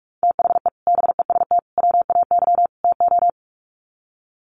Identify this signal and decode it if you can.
Morse code, heard as a keyed tone, about 44 wpm, 700 Hz. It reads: THE BEST WAY TO
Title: Sets of 4 Words 45wpm